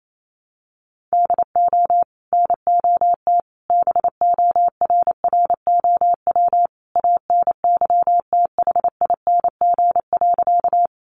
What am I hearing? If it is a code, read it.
Morse code, heard as a keyed tone, about 28 wpm, 700 Hz. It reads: DO NOT BORROW ANYTHING.
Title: Sentences from Top 1500 Words 28wpm